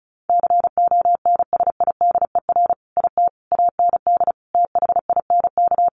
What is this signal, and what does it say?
Morse code, heard as a keyed tone, about 34 wpm, 700 Hz. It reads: CONSIDER IT AND THINK
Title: Sentences from Top 1400 Words 35wpm